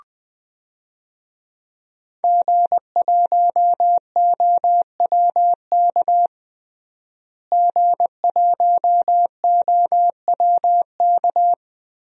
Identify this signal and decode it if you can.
Morse code, heard as a keyed tone, about 20 wpm, 700 Hz. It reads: G1OWK G1OWK
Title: DXCC-100 Call Signs - 5 Characters - Repeated 3x - 20wpm